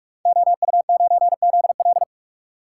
Morse code, heard as a keyed tone, about 44 wpm, 700 Hz. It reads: OU9ZL
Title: Code Groups - 5 Letter:Numbers - 45wpm